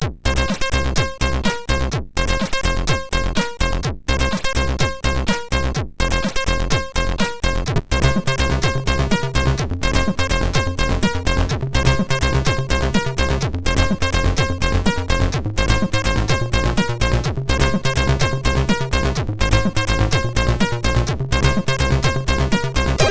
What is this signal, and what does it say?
There is no code.